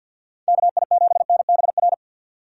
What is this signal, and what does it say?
Morse code, heard as a keyed tone, about 50 wpm, 700 Hz. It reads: KI7NBR
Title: USA Call Signs - 6 Characters - 4x Character Spacing - 50wpm